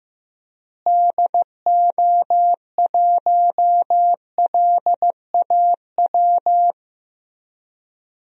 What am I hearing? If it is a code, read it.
Morse code, heard as a keyed tone, about 15 wpm, 700 Hz. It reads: DO1LAW
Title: DXCC-100 Call Signs - 6 Characters - Repeated 2x - 15wpm